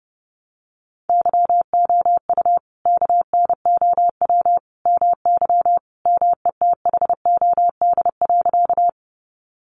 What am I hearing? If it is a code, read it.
Morse code, heard as a keyed tone, about 28 wpm, 700 Hz. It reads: YOU KNOW MY METHOD.
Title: Sentences from Top 600 Words 30wpm